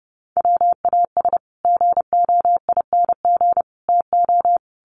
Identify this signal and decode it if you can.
Morse code, heard as a keyed tone, about 30 wpm, 700 Hz. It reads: WAS GOING TO
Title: Sets of 5 Words Encore 30wpm